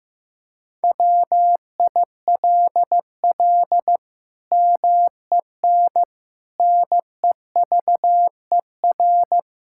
Morse code, heard as a keyed tone, about 15 wpm, 700 Hz. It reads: WILL MEN NEVER
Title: Sentences from Top 1600 Words 15wpm